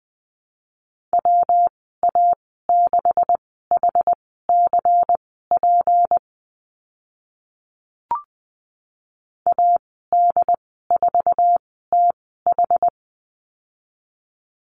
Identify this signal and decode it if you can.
Morse code, heard as a keyed tone, about 20 wpm, spaced as for 15 wpm, 700 Hz. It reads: WA6HCP AD4TH
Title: 161 Lesson 26g - Letter g - Call Signs 20wpm